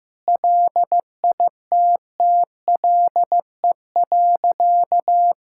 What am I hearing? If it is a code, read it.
Morse code, heard as a keyed tone, about 14 wpm, 700 Hz. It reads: LITTLE.
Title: Sentences from Top 1600 Words 15wpm